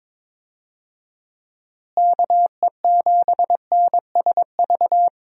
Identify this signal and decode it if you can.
Morse code, heard as a keyed tone, about 22 wpm, 700 Hz. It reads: KE7NSV
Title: USA Call Signs - 6 Characters - Repeated 2x - 22wpm